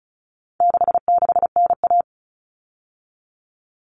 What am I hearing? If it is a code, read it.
Morse code, heard as a keyed tone, about 34 wpm, 700 Hz. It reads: 66NA
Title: All Cut Numbers - 4-Digit 35wpm